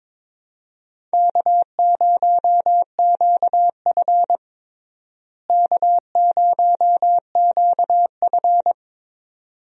Morse code, heard as a keyed tone, about 22 wpm, 700 Hz. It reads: K0QF K0QF
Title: USA Call Signs - 4 Characters - Repeated 2x - 22wpm